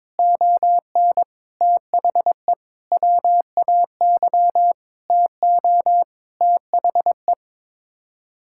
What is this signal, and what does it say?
Morse code, heard as a keyed tone, about 22 wpm, 700 Hz. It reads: ON THE WAY TO THE
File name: Sets of 9 Words - 22wpm